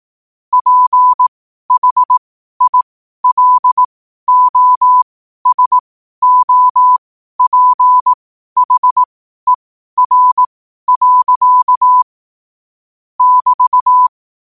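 Morse code, heard as a keyed tone, about 18 wpm, spaced as for 12 wpm, 1000 Hz. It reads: PHILOSOPHER. <BT>
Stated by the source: Morse code sent at 12 words per minute.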